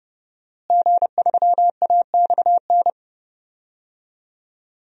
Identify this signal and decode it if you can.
Morse code, heard as a keyed tone, about 30 wpm, 700 Hz. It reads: G3AXN
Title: DXCC-100 Call Signs - 5 Characters - Repeated 3x - 30wpm